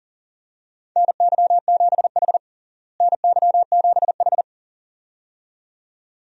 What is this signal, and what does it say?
Morse code, heard as a keyed tone, about 40 wpm, 700 Hz. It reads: NY7H NY7H
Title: USA Call Signs - 4 Characters - Repeated 2x - 40wpm